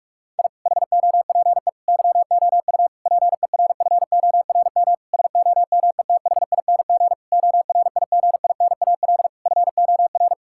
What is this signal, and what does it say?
Morse code, heard as a keyed tone, about 45 wpm, 700 Hz. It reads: I HOPE YOU PERFORM SOMETHING ORIGINAL FOR
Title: Sentences from Top 1000 Words - Review Part 6 - 45wpm